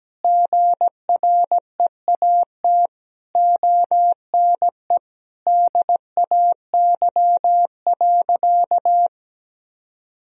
Morse code, heard as a keyed tone, about 17 wpm, 700 Hz. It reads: GREAT ONE DAY.
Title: Sentences from Top 1000 Words - Review Part 1 - 17wpm